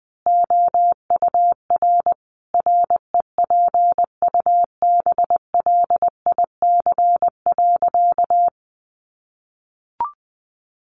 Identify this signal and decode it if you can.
Morse code, heard as a keyed tone, about 20 wpm, 700 Hz. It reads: OUR REPUBLIC.
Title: Sentences from Top 1500 Words 20wpm